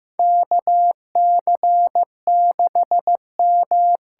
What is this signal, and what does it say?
Morse code, heard as a keyed tone, about 15 wpm, 700 Hz. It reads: KC6M
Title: USA Call Signs - 6 Characters - 8x Character Spacing - 15wpm